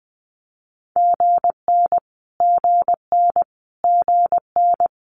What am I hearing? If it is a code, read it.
Morse code, heard as a keyed tone, about 20 wpm, 700 Hz. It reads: GN GN GN
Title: Multi-letter Phonemes Repeated 3x 20wpm